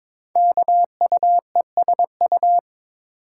Morse code, heard as a keyed tone, about 22 wpm, 690 Hz. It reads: KUESU
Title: Code Groups - 5 Letter:Numbers - 22wpm